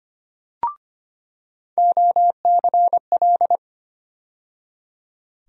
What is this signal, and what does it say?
Morse code, heard as a keyed tone, about 25 wpm, 700 Hz. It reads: OCL
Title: USA Call Sign Postfixes - 3 Letters 25wpm